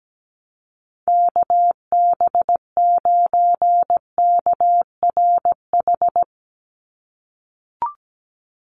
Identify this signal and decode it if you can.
Morse code, heard as a keyed tone, about 17 wpm, 700 Hz. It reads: KB9KRH
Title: USA Call Signs 17wpm